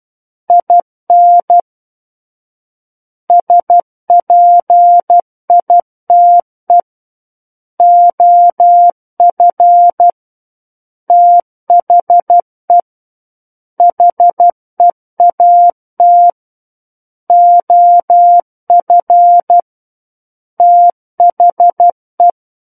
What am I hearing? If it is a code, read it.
Morse code, heard as a keyed tone, about 12 wpm, 700 Hz. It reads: IN SPITE OF THE HEAT OF THE
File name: Book One / 03-Chapter 3 - 12 WPM